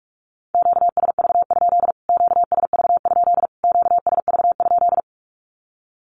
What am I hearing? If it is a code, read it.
Morse code, heard as a keyed tone, about 46 wpm, 700 Hz. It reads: QSV? QSV? QSV?
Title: Q-Codes Repeated 3x 45wpm